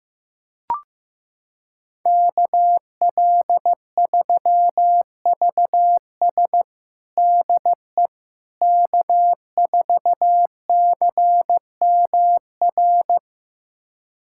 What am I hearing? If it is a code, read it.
Morse code, heard as a keyed tone, about 15 wpm, 700 Hz. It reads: KL3VS DE K4CMR
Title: USA Callsign DE Callsign - 5 Characters - 15wpm